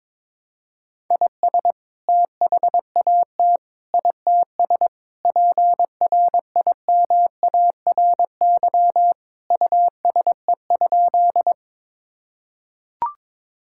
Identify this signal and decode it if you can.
Morse code, heard as a keyed tone, about 22 wpm, 700 Hz. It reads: IS THAT ITS PRIMARY USE?
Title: Sentences from Top 1000 Words - Review Part 5 - 22wpm